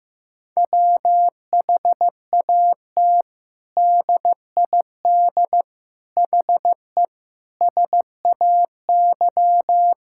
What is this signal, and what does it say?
Morse code, heard as a keyed tone, about 15 wpm, 700 Hz. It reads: WHAT DID HE SAY
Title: Sentences from Top 1000 Words - Review Part 6 - 15wpm